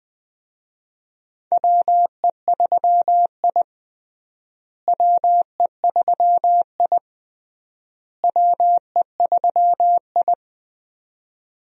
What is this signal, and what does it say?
Morse code, heard as a keyed tone, about 21 wpm, 700 Hz. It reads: WE3I WE3I WE3I
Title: USA Call Signs - 4 Characters - Repeated 3x - 20wpm